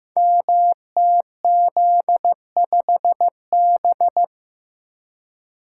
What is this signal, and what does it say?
Morse code, heard as a keyed tone, about 15 wpm, 700 Hz. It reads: MTZ5B
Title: Code Groups - 5 Letter:Numbers - 15wpm